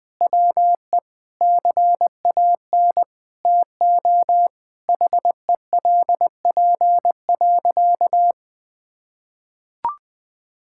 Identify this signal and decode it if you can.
Morse code, heard as a keyed tone, about 20 wpm, 700 Hz. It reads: WE CAN TO HELP.
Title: Sentences with 7-Word Phrases - Spoken Beginning - 20wpm